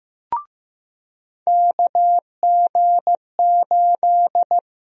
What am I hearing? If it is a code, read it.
Morse code, heard as a keyed tone, about 15 wpm, 700 Hz. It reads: KG8
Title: USA Call Sign Prefixes - Repeated 2x - 15wpm